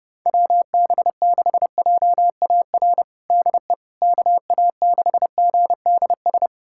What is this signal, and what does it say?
Morse code, heard as a keyed tone, about 31 wpm, 700 Hz. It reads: WB6JAR DE KA6GDS
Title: USA Callsign DE Callsign - 6 Characters - 30wpm